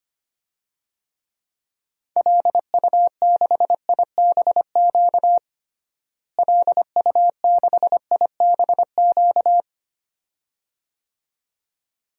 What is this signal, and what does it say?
Morse code, heard as a keyed tone, about 25 wpm, 700 Hz. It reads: LU6IBQ LU6IBQ
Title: DXCC-100 Call Signs - 6 Characters - Repeated 2x - 25wpm